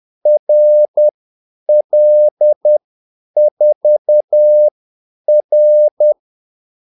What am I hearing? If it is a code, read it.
Morse code, heard as a keyed tone, about 10 wpm, spaced as for 8 wpm, 600 Hz. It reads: RL4R
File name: Callsigns-10-8wpm